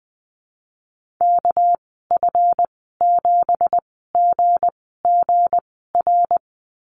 Morse code, heard as a keyed tone, about 20 wpm, spaced as for 14 wpm, 700 Hz. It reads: KF7GGR
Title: 213 Lesson 34g - Number 8 - Call Signs 20wpm